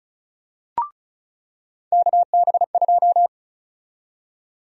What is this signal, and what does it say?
Morse code, heard as a keyed tone, about 36 wpm, 700 Hz. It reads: KB2
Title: USA Call Sign Prefixes - 35wpm